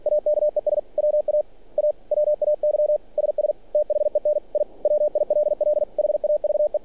Morse code, heard as a keyed tone, about 46 wpm, 600 Hz. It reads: AKES WA AWAY SS THEN I WILL HAVE
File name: archived / CRSnet